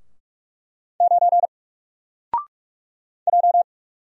Morse code, heard as a keyed tone, about 44 wpm, 700 Hz. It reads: 9 J
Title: Single Character (Letter:Number:Punctuation:Prosign) 45wpm